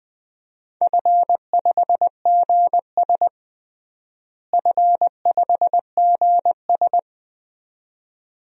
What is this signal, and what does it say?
Morse code, heard as a keyed tone, about 20 wpm, 700 Hz. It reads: F5GS F5GS
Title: DXCC-100 Call Signs - 4 Characters - Repeated 2x - 20wpm